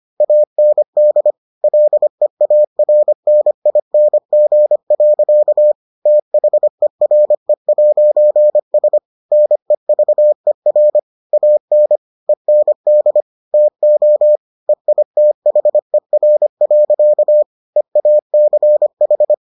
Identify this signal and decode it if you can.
Morse code, heard as a keyed tone, about 25 wpm, 600 Hz. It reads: AND LEARNING. THERE'S NEVER AN END TO EITHER. EACH